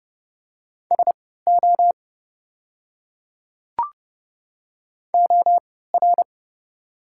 Morse code, heard as a keyed tone, about 30 wpm, spaced as for 17 wpm, 700 Hz. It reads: SO OR
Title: 3x Character Spacing - Top 100 Words - 30wpm